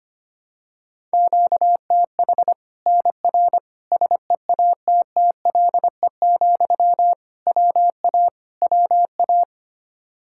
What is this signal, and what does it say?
Morse code, heard as a keyed tone, about 25 wpm, 700 Hz. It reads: QTH NR SEATTLE, WA WA
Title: Practice QSO - Exchange 1 - Beginning - 25wpm